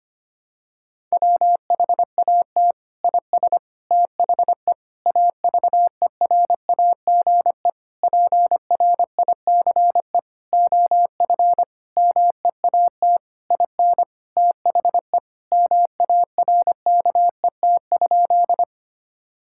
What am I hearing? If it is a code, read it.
Morse code, heard as a keyed tone, about 25 wpm, 700 Hz. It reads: WHAT IS THE AVERAGE PRICE OF MEAT IN THE MARKET?
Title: Sentences from Top 1000 Words - Review Part 6 - 25wpm